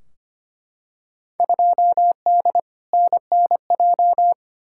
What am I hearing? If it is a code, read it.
Morse code, heard as a keyed tone, about 25 wpm, 700 Hz. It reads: 2D NNJ
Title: Field Day Exchanges - 25wpm